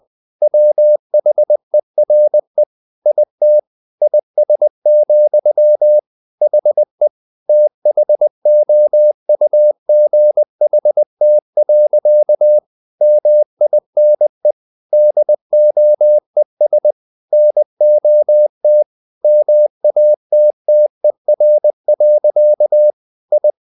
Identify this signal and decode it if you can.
Morse code, heard as a keyed tone, about 20 wpm, 600 Hz. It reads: WHERE IT IS, HE THOUGHT. MINE DOES NOT MATTER. I
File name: the old man and the sea part 2 20wpm